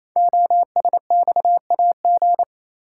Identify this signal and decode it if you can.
Morse code, heard as a keyed tone, about 27 wpm, 700 Hz. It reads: OSXAG
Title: Code Groups - 5 Letters - 28wpm